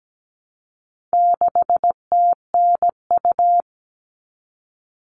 Morse code, heard as a keyed tone, about 17 wpm, 700 Hz. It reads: 6TNU
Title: All Cut Numbers - 4-Digit 17wpm